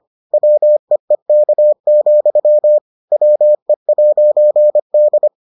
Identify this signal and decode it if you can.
Morse code, heard as a keyed tone, about 25 wpm, 600 Hz. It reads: WEEK, WE'D